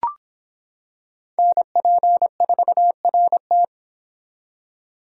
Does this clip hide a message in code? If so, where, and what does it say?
Yes: Morse code from 1.4 to 3.6 s, heard as a keyed tone, about 25 wpm, 700 Hz. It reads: NP4RT